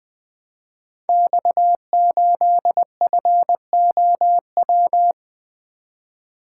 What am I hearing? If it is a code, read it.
Morse code, heard as a keyed tone, about 20 wpm, 700 Hz. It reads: X8FOW
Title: Code Groups - 5 Letter:Numbers - 20wpm